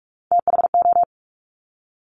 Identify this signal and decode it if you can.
Morse code, heard as a keyed tone, about 46 wpm, 700 Hz. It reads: THO
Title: Top Three Letter Combinations in Words 45wpm